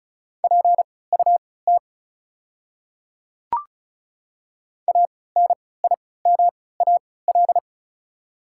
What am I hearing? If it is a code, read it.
Morse code, heard as a keyed tone, about 36 wpm, spaced as for 20 wpm, 700 Hz. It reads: PUT ANIMAL
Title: 3x Character Spacing - Top 200 Words - 35wpm